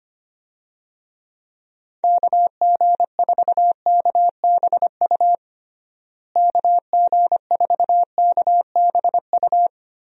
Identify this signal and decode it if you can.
Morse code, heard as a keyed tone, about 25 wpm, 700 Hz. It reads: KG4KBU KG4KBU
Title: USA Call Signs - 6 Characters - Repeated 2x - 25wpm